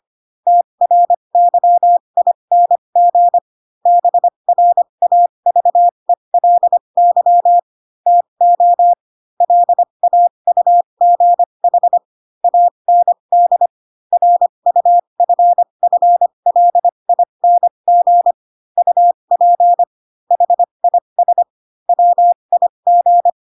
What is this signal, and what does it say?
Morse code, heard as a keyed tone, about 25 wpm, spaced as for 22 wpm, 700 Hz. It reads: TRYING BRAVELY TO LAUGH AND RUFFLING UP HIS WIG